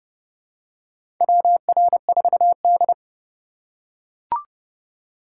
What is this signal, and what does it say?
Morse code, heard as a keyed tone, about 30 wpm, 700 Hz. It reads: WR4D
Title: USA Call Signs - 6x Character Spacing - 30wpm